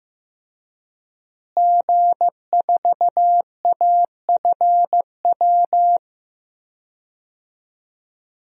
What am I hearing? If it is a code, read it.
Morse code, heard as a keyed tone, about 15 wpm, 700 Hz. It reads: G4AFW
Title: DXCC-100 Call Signs - 5 Characters - Repeated 3x - 15wpm